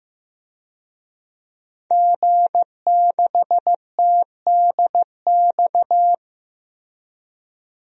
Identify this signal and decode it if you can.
Morse code, heard as a keyed tone, about 15 wpm, 700 Hz. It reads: G6TDX
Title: DXCC-100 Call Signs - 5 Characters - Repeated 2x - 15wpm